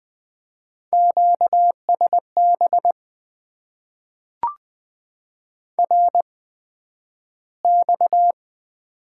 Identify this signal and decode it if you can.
Morse code, heard as a keyed tone, about 20 wpm, spaced as for 11 wpm, 700 Hz. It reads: QSB RX
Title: Top 90 QSO Elements - 8x Character Spacing - 20wpm